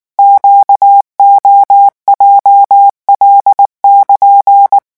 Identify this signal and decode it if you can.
Morse code, heard as a keyed tone, about 19 wpm, 800 Hz. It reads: QOJL<KN>
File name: morse mp3 / revision15-vitesse 18 mots